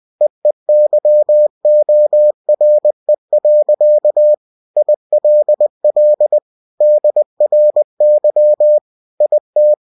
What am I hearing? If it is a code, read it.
Morse code, heard as a keyed tone, about 20 wpm, 600 Hz. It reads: EEYORE. ILL DRY IT
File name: cw-books / winnie-the-pooh / winnie-20wpm-page-0088